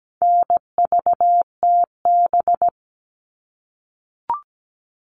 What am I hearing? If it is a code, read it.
Morse code, heard as a keyed tone, about 17 wpm, 700 Hz. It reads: NVTB
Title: All Cut Numbers - 4-Digit 17wpm